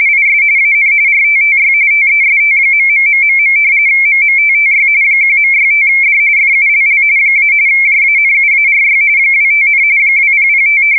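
RTTY, about 45.45 baud. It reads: IS AN RTTY TEST..... MMTTY 1.65D FULL - JANUARY 25, 2005
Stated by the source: RTTY test recording for the JE3HHT MMTTY engine used with UcxLog logging software.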